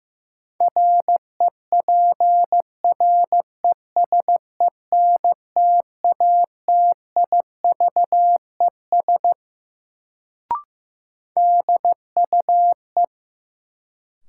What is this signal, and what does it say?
Morse code, heard as a keyed tone, about 15 wpm, 700 Hz. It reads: REPRESENTATIVES DUE
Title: Sets of 5 Words Encore - New Word Drill - 15wpm